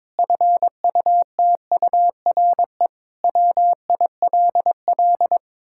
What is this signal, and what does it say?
Morse code, heard as a keyed tone, about 22 wpm, 700 Hz. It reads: FUTURE WILL
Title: Sentences from Top 1000 Words - Review Part 5 - 22wpm